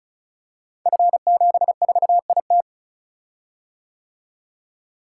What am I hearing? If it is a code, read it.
Morse code, heard as a keyed tone, about 36 wpm, 690 Hz. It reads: F74IT
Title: Code Groups - 5 Letter:Numbers - 35wpm